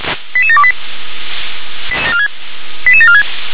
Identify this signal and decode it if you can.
Five-tone selective call (ZVEI), heard as quick runs of five tones, tones of about 70 ms. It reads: 88728 88746
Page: SDR/ZVEI